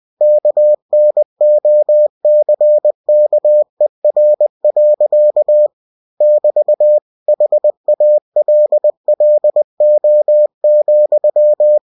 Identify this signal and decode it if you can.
Morse code, heard as a keyed tone, about 20 wpm, 600 Hz. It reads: KNOCKER. <BT> HALLO,
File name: cw-books / winnie-the-pooh / winnie-20wpm-page-0048